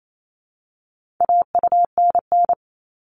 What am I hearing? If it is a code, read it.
Morse code, heard as a keyed tone, about 28 wpm, 700 Hz. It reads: AUNN
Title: All Cut Numbers - 4-Digit 28wpm